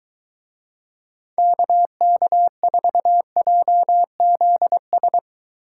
Morse code, heard as a keyed tone, about 21 wpm, 700 Hz. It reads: KK4JZS